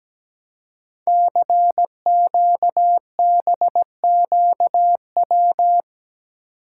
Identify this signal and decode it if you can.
Morse code, heard as a keyed tone, about 17 wpm, 700 Hz. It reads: CQBQW
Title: Code Groups - 5 Letter:Numbers - 17wpm